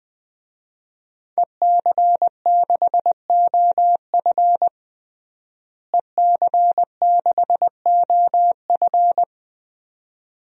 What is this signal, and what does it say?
Morse code, heard as a keyed tone, about 21 wpm, 700 Hz. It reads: EC6OF EC6OF
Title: DXCC-100 Call Signs - 5 Characters - Repeated 2x - 20wpm